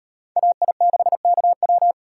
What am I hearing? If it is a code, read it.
Morse code, heard as a keyed tone, about 38 wpm, 700 Hz. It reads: AI6KW